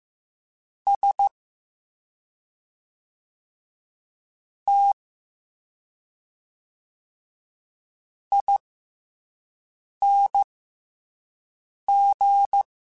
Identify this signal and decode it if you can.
Morse code, heard as a keyed tone, about 15 wpm, spaced as for 5 wpm, 800 Hz. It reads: S T ING